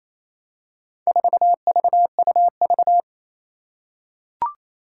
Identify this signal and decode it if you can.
Morse code, heard as a keyed tone, about 28 wpm, 700 Hz. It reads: 4VUV
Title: Four Challenging Letter-Number Combinations 28wpm